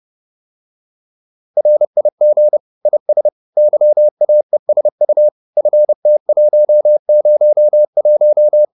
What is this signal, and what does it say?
Morse code, heard as a keyed tone, about 30 wpm, 600 Hz. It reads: RIG IS YAESU FT101